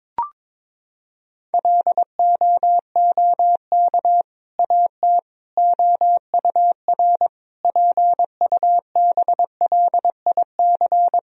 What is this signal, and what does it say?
Morse code, heard as a keyed tone, about 22 wpm, 700 Hz. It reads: LOOK AT OUR PUBLIC
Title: Sentences from Top 1000 Words - Review Part 3 - 22wpm